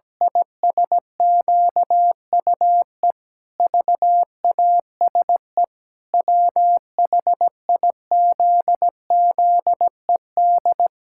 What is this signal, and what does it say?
Morse code, heard as a keyed tone, about 17 wpm, 700 Hz. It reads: ISQUE VASE WHIZZED
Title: Pangram Challenge v2 - 17wpm